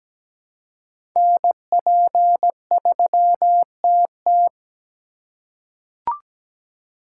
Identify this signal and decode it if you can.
Morse code, heard as a keyed tone, about 17 wpm, 700 Hz. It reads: NP3TT
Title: USA Call Signs - 4x Character Spacing - 17wpm